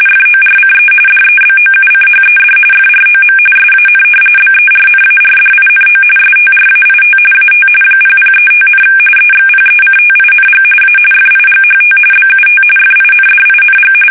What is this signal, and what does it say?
RTTY, about 45.45 baud. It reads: FORCE DEBOUCHED FROM SAN BERNARDINO STRAIT AT 0035, EXPECTING A FIGHT BUT FINDING
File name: fsk850 / battleshipnj